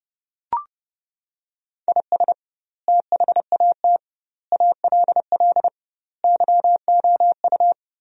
Morse code, heard as a keyed tone, about 30 wpm, 700 Hz. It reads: IS THAT ALL YOU
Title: 2x Word-Spacing - Sentences from Top 200 Words - 30wpm